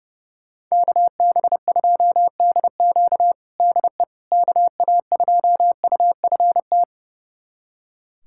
Morse code, heard as a keyed tone, about 30 wpm, 700 Hz. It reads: KB2DQ DE KA2UFT
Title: USA Callsign DE Callsign 30wpm